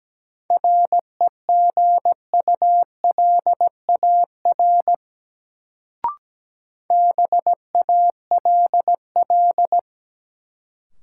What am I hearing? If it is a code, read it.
Morse code, heard as a keyed tone, about 17 wpm, 700 Hz. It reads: REGULAR BALL
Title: Sentences from Top 900 Words - New Word Drill - 17wpm